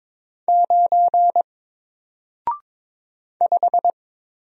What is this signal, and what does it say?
Morse code, heard as a keyed tone, about 22 wpm, 700 Hz. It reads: 9 5
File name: Single-Letter-Number-22wpm